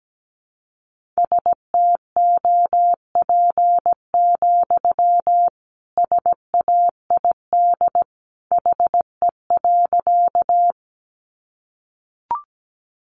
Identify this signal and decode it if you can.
Morse code, heard as a keyed tone, about 17 wpm, 700 Hz. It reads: STOP, SAID HE.
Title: Sentences from Top 300 Words 17wpm